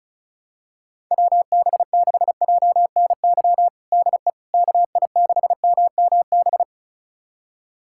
Morse code, heard as a keyed tone, about 36 wpm, 700 Hz. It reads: WB6JNY DE KI6MMB
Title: USA Callsign DE Callsign 35wpm